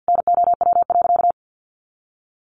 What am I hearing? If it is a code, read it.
Morse code, heard as a keyed tone, about 50 wpm, 700 Hz. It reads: NOW.
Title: Sentences from Top 1200 Words 50wpm